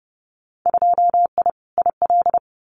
Morse code, heard as a keyed tone, about 30 wpm, 700 Hz. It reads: 2I IL
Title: Winter Field Day Exchanges 30wpm